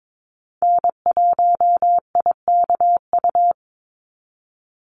Morse code, heard as a keyed tone, about 22 wpm, 700 Hz. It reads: N1IKU
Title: USA Call Signs 22wpm